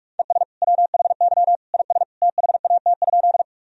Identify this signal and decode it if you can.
Morse code, heard as a keyed tone, about 46 wpm, 700 Hz. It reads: ES WHY IS THAT?
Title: Sentences from Top 200 Words and Abbreviations 45wpm